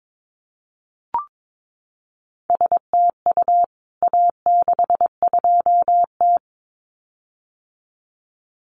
Morse code, heard as a keyed tone, about 22 wpm, 700 Hz. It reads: STU A62T
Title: CWT Exchange 22wpm